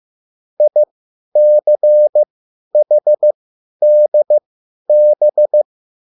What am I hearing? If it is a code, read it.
Morse code, heard as a keyed tone, about 15 wpm, spaced as for 10 wpm, 600 Hz. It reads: ICHDB
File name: QSL / CW lesson